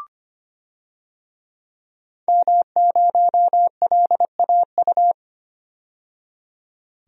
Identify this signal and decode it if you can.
Morse code, heard as a keyed tone, about 25 wpm, 700 Hz. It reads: M0LAU
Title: DXCC-100 Call Signs - 5 Characters - 25wpm